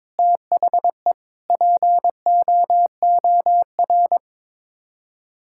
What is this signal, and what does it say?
Morse code, heard as a keyed tone, about 21 wpm, 700 Hz. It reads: THE POOR
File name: practice / 22wpm / Sets of 8 Words - 22wpm